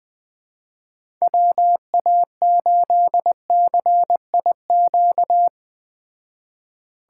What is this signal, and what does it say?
Morse code, heard as a keyed tone, about 20 wpm, 700 Hz. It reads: WA8CIQ